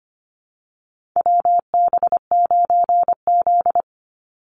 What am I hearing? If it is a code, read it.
Morse code, heard as a keyed tone, about 25 wpm, 700 Hz. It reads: WB9Z
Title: Call Signs Repeated 2x - 25wpm